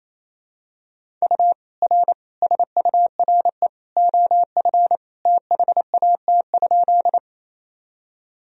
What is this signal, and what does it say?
Morse code, heard as a keyed tone, about 28 wpm, 700 Hz. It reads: U R SURE OF THAT?
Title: Sentences from Top 300 Words and Abbreviations 28wpm